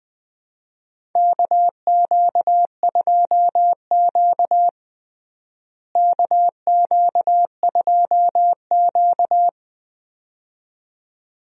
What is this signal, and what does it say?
Morse code, heard as a keyed tone, about 20 wpm, 700 Hz. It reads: KQ2Q KQ2Q
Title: USA Call Signs - 4 Characters - Repeated 2x - 20wpm